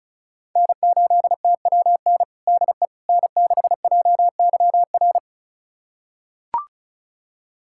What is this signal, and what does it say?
Morse code, heard as a keyed tone, about 34 wpm, 700 Hz. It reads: N8TWN DE N6JYR
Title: USA Callsign DE Callsign 35wpm